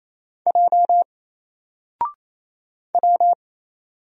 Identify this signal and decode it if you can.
Morse code, heard as a keyed tone, about 27 wpm, 700 Hz. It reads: J W
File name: Single-Letter-Number-28wpm